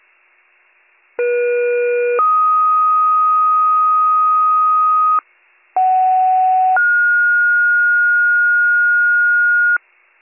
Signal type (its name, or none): two-tone page